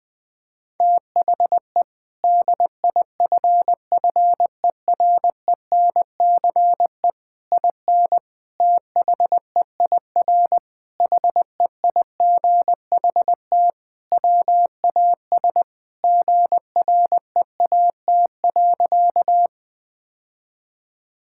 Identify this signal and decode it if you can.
Morse code, heard as a keyed tone, about 20 wpm, 700 Hz. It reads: THE DIFFERENCE IN THEIR HEIGHT WAS GREAT.
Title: Sentences from Top 1600 Words 20wpm